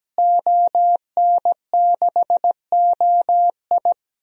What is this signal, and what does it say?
Morse code, heard as a keyed tone, about 17 wpm, 700 Hz. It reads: ON6OI
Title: DXCC-100 Call Signs - 5 Characters - Repeated 2x - 17wpm